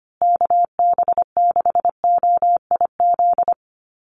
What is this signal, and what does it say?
Morse code, heard as a keyed tone, about 25 wpm, 700 Hz. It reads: KB6OIZ
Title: USA Call Signs Repeated 2x 25wpm